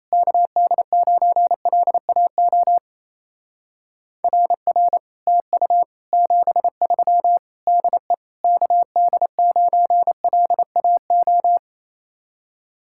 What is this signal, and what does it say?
Morse code, heard as a keyed tone, about 36 wpm, 700 Hz. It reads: KD9LAO RR TU 73 DE KD9LAO